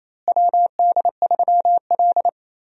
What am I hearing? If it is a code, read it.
Morse code, heard as a keyed tone, about 28 wpm, 700 Hz. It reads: WD3L
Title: USA Call Signs - 4 Characters - Repeated 2x - 28wpm